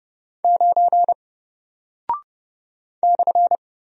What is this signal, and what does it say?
Morse code, heard as a keyed tone, about 29 wpm, 700 Hz. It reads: 9 /
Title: Single Character (Letter:Number:Punctuation:Prosign) Alt 30wpm